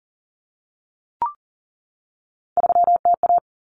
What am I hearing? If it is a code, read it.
Morse code, heard as a keyed tone, about 40 wpm, 700 Hz. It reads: 3TA
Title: Common Cut Numbers - 3-Digit 40wpm